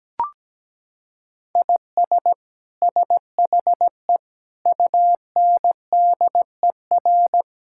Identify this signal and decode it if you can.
Morse code, heard as a keyed tone, about 17 wpm, 700 Hz. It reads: IS SHE UNDER
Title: Sentences from Top 1600 Words 17wpm